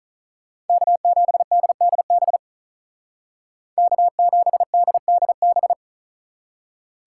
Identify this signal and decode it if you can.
Morse code, heard as a keyed tone, about 46 wpm, 700 Hz. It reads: K7DDB K7DDB